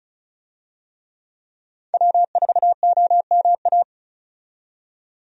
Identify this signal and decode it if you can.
Morse code, heard as a keyed tone, about 36 wpm, 700 Hz. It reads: W4OMA
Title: USA Call Signs - 5 Characters - 35wpm